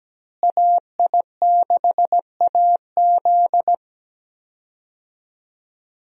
Morse code, heard as a keyed tone, about 17 wpm, 700 Hz. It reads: AI6AZ
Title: USA Call Signs - 5 Characters - Repeated 2x - 17wpm